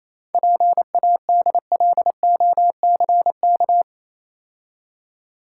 Morse code, heard as a keyed tone, about 28 wpm, 700 Hz. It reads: PADLOCK